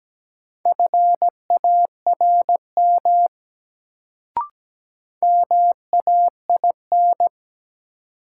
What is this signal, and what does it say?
Morse code, heard as a keyed tone, about 17 wpm, 700 Hz. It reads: FARM MAIN
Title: Sentences from Top 1000 Words - Review Part 2 - New Word Drill - 17wpm